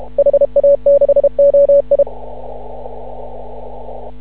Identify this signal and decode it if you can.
Morse code, heard as a keyed tone, about 36 wpm, 560 Hz. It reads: HA6OI
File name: CW / ha6oi 2